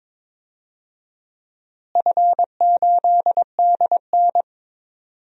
Morse code, heard as a keyed tone, about 22 wpm, 700 Hz. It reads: F8DN
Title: DXCC-100 Call Signs - 4 Characters - Repeated 2x - 22wpm